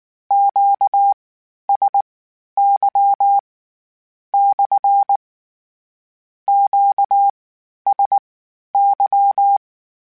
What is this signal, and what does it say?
Morse code, heard as a keyed tone, about 19 wpm, spaced as for 11 wpm, 800 Hz. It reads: QSY / QSY
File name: Q Calls / Q Calls-mp3s / Q CALLS SPSL @18